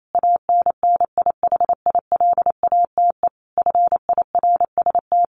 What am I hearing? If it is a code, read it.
Morse code, heard as a keyed tone, about 28 wpm, 700 Hz. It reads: ANNIHILATE FIRST
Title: A Sound Of Thunder 28wpm